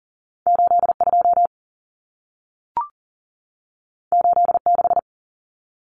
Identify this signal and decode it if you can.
Morse code, heard as a keyed tone, about 40 wpm, 700 Hz. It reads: 82 86
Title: Numbers - 2-Digit 40wpm